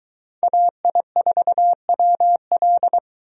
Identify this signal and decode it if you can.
Morse code, heard as a keyed tone, about 23 wpm, 700 Hz. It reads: AI4WL